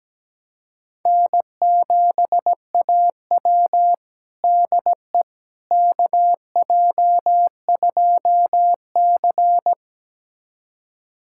Morse code, heard as a keyed tone, about 17 wpm, 700 Hz. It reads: N7AW DE KJ2C
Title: USA Callsign DE Callsign - 4 Characters - 17wpm